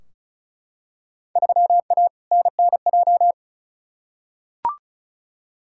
Morse code, heard as a keyed tone, about 36 wpm, 700 Hz. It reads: 3A NNJ
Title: Field Day Exchanges - 35wpm